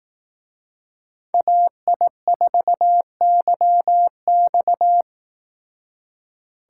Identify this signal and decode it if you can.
Morse code, heard as a keyed tone, about 18 wpm, 700 Hz. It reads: AI4YX
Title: Speed Racing - USA Call Signs 15wpm